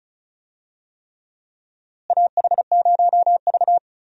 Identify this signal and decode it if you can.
Morse code, heard as a keyed tone, about 36 wpm, 700 Hz. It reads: AH0V
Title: USA Call Signs - 4 Characters - Repeated 3x - 35wpm